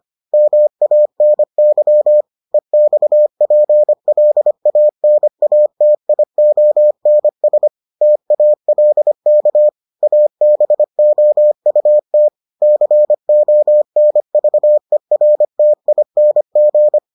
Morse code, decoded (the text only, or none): MANY EXPLANATIONS TALK ABOUT CONVERTING